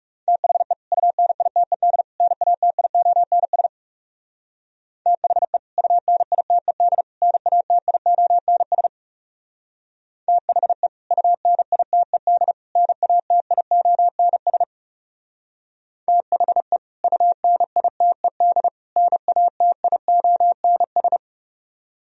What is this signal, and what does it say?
Morse code, heard as a keyed tone, about 38 wpm, 700 Hz. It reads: THE UNITED NATIONS THE UNITED NATIONS THE UNITED NATIONS THE UNITED NATIONS